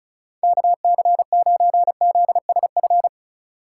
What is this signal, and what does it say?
Morse code, heard as a keyed tone, about 34 wpm, 700 Hz. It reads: KC9ZSF
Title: USA Call Signs - 6 Characters - 4x Character Spacing - 35wpm